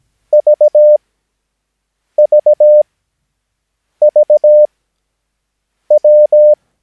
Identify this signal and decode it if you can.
Morse code, heard as a keyed tone, about 17 wpm, spaced as for 6 wpm, 600 Hz. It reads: VVVW
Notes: Övningsmaterial med alfabetet A till Ö samt siffror 0 till 9 enligt International/Continental Morse Code.